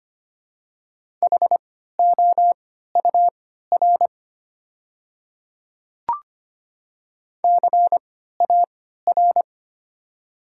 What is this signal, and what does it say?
Morse code, heard as a keyed tone, about 25 wpm, spaced as for 14 wpm, 700 Hz. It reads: HOUR CAR
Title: 3x Character Spacing - Top 500 Words - 25wpm